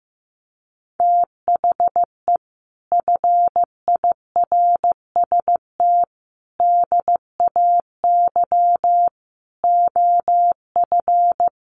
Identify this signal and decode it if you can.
Morse code, heard as a keyed tone, about 15 wpm, 700 Hz. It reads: THE FIRST DAY OF
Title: Sets of 5 Words Encore2 15wpm